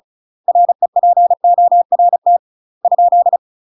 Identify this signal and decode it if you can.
Morse code, heard as a keyed tone, about 36 wpm, 700 Hz. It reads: REPORT ?
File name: UK MASTER QSOs TRAINING/35WPM